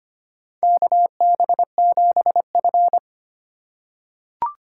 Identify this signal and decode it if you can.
Morse code, heard as a keyed tone, about 25 wpm, 700 Hz. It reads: KB7F
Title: USA Call Signs - 4 Characters - 3x Character Spacing - 25wpm